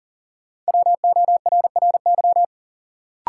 Morse code, heard as a keyed tone, about 39 wpm, 700 Hz. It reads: WORRY
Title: Sentences from Top 1100 Words - New Word Drill - 40wpm